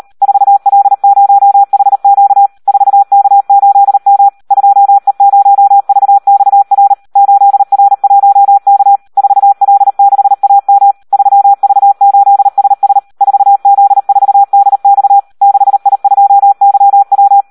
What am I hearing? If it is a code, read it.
Morse code, heard as a keyed tone, about 38 wpm, 800 Hz. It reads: 4L0HQ 4K8M 2E0VXR 8R1K 4L6AM 3V8SS 4Z4DX 6I2YW
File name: Audio practica CW / SD 1 38